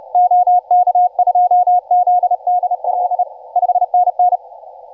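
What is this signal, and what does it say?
Morse code, heard as a keyed tone, about 32 wpm, 720 Hz, with radio noise. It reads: OK2ZDL 5NN
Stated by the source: Morse from DX station 5T5OK on 20 m CW, posted by IZ8GNR.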